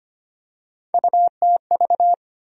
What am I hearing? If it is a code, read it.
Morse code, heard as a keyed tone, about 25 wpm, 700 Hz. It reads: UTV
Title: USA Call Sign Postfixes - 3 Letters 25wpm